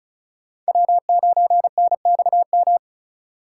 Morse code, heard as a keyed tone, about 39 wpm, 700 Hz. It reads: W9NXM
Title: Speed Racing - USA Call Signs 35wpm